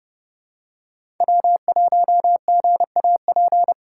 Morse code, heard as a keyed tone, about 30 wpm, 700 Hz. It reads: W1GAP
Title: USA Call Signs - 2x Character Spacing - 30wpm